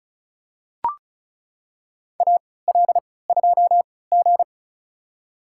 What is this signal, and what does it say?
Morse code, heard as a keyed tone, about 36 wpm, spaced as for 20 wpm, 700 Hz. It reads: AL2G
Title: USA Call Signs - 4 Characters - 3x Character Spacing - 35wpm